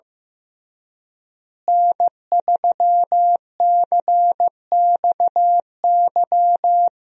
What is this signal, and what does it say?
Morse code, heard as a keyed tone, about 16 wpm, 700 Hz. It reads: N3CXY
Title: USA Call Signs - 5 Characters - Repeated 3x - 15wpm